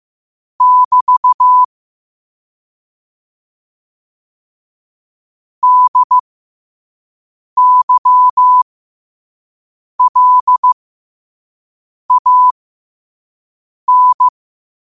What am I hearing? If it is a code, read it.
Morse code, heard as a keyed tone, about 15 wpm, spaced as for 5 wpm, 1000 Hz. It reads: <BT> DYLAN
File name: 5wpm